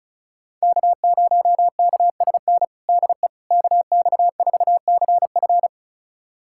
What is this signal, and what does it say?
Morse code, heard as a keyed tone, about 36 wpm, 700 Hz. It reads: K0KSN DE KX4CF
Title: USA Callsign DE Callsign 35wpm